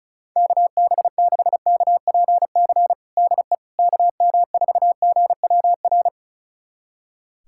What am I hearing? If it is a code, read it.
Morse code, heard as a keyed tone, about 36 wpm, 700 Hz. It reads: KB6KPC DE KM4GWR
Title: USA Callsign DE Callsign - 6 Characters - 35wpm